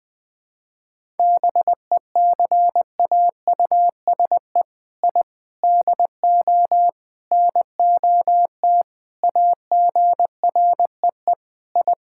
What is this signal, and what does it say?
Morse code, heard as a keyed tone, about 20 wpm, 700 Hz. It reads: BECAUSE I DO NOT AGREE I
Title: Sentences with 4-Word Phrases - Spoken Beginning - 20wpm